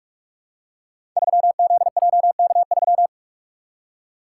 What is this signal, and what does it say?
Morse code, heard as a keyed tone, about 44 wpm, 700 Hz. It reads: 3ZJK3
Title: Code Groups - 5 Letter:Numbers - 45wpm